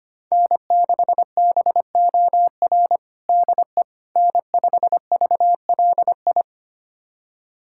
Morse code, heard as a keyed tone, about 25 wpm, 700 Hz. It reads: N6BOR DE N5VLI
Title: USA Callsign DE Callsign - 5 Characters - 25wpm